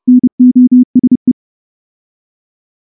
Morse code, heard as a keyed tone, about 28 wpm, 250 Hz. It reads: NOSE
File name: morse-training-30WPM